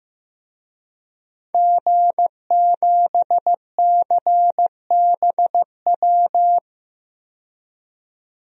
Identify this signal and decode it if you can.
Morse code, heard as a keyed tone, about 15 wpm, 700 Hz. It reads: G7CBW
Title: DXCC-100 Call Signs - 5 Characters - 15wpm